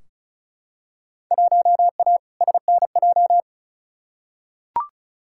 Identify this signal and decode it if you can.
Morse code, heard as a keyed tone, about 36 wpm, 700 Hz. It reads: 1A SNJ
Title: Field Day Exchanges - 35wpm